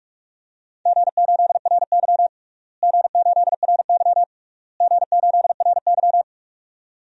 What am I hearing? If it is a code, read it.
Morse code, heard as a keyed tone, about 45 wpm, 700 Hz. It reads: G8RY G8RY G8RY
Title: DXCC-100 Call Signs - 4 Characters - Repeated 3x - 45wpm